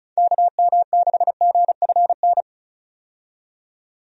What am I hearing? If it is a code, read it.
Morse code, heard as a keyed tone, about 34 wpm, 700 Hz. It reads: KM6GFN
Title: USA Call Signs - 6 Characters - 35wpm